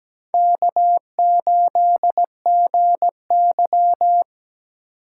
Morse code, heard as a keyed tone, about 20 wpm, 700 Hz. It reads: K8GY
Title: Speed Racing - USA 1x2 Call Signs 17wpm